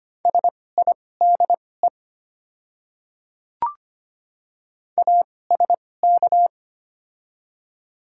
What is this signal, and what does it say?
Morse code, heard as a keyed tone, about 25 wpm, spaced as for 18 wpm, 700 Hz. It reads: SIDE ASK
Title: 2x Character Spacing - Top 300 Words - 25wpm